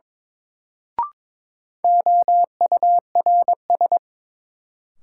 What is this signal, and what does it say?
Morse code, heard as a keyed tone, about 22 wpm, 700 Hz. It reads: OURS
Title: Sentences from Top 1300 Words - New Word Drill Alt - 22wpm